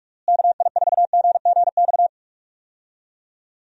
Morse code, heard as a keyed tone, about 46 wpm, 700 Hz. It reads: KI4GGX
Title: USA Call Signs - 6 Characters - Repeated 3x - 45wpm